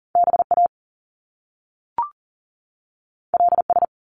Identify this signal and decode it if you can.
Morse code, heard as a keyed tone, about 42 wpm, 700 Hz. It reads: BA LS
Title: Top Two Letter Combinations in Words 40wpm